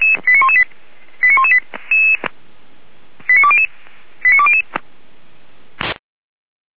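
Five-tone selective call (ZVEI), heard as quick runs of five tones, tones of about 70 ms. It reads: 89118 89118 89200 89200
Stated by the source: ZVEI five-tone selective call received on SDR.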